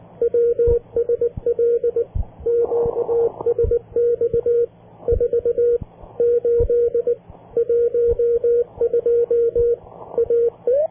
Morse code, heard as a keyed tone, about 19 wpm, 470 Hz. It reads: WSL QSX 4 8 12 AE